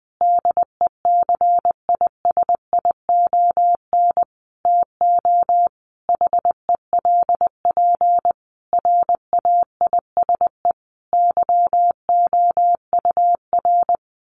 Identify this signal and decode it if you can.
Morse code, heard as a keyed tone, about 20 wpm, 700 Hz. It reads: DECISION TO HELP RAISE YOUR
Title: Sentences from Top 1400 Words 20wpm